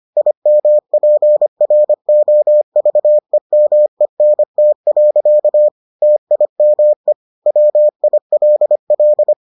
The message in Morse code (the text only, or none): IMPROVEMENT. TIME WILL